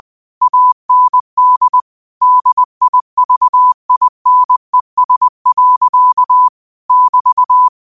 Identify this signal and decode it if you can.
Morse code, heard as a keyed tone, about 20 wpm, 1000 Hz. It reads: AND DIVINES. <BT>
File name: qotdfiles / 20wpm